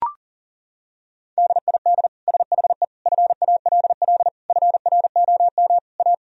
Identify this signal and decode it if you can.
Morse code, heard as a keyed tone, about 37 wpm, 700 Hz. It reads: DID SHE FALL FROM A
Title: Sentences from Top 1000 Words - Review Part 2 - 40wpm